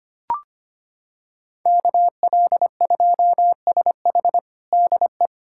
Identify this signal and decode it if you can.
Morse code, heard as a keyed tone, about 25 wpm, 700 Hz. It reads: KL2SH DE
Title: USA Callsign DE Callsign 25wpm